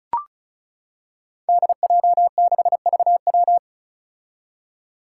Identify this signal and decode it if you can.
Morse code, heard as a keyed tone, about 34 wpm, 700 Hz. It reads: DJ6VW
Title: Five Challenging Letter-Number Combinations 35wpm